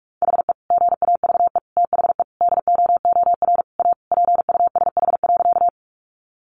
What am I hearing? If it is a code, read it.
Morse code, heard as a keyed tone, about 44 wpm, 700 Hz. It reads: HE GAVE THE DOOR A PUSH.
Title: Sentences from Top 1300 Words 45wpm